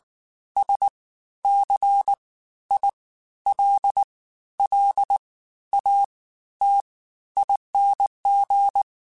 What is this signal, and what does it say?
Morse code, heard as a keyed tone, about 19 wpm, spaced as for 12 wpm, 800 Hz. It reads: SCILLATING
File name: Special INGs-mp3s / Special INGs-mp3s / Fast ing @18